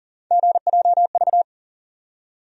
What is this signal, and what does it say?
Morse code, heard as a keyed tone, about 40 wpm, 700 Hz. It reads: GJV
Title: USA Call Sign Postfixes - 3 Letters 40wpm